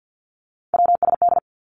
Morse code, heard as a keyed tone, about 50 wpm, 700 Hz. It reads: WSD
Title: Three Challenging Letter-Number Combinations 50wpm